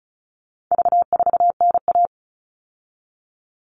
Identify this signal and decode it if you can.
Morse code, heard as a keyed tone, about 34 wpm, 700 Hz. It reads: V4NA
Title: All Cut Numbers - 4-Digit 35wpm